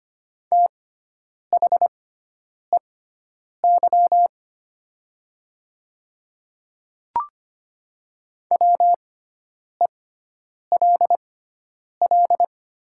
Morse code, heard as a keyed tone, about 25 wpm, spaced as for 9 wpm, 690 Hz. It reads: THEY WELL
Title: 6x Character Spacing - Top 200 Words - 25wpm